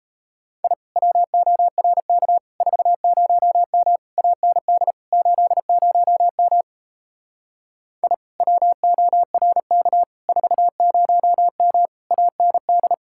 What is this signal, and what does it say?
Morse code, heard as a keyed tone, about 37 wpm, 700 Hz. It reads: I WORK 40M AND 80M I WORK 40M AND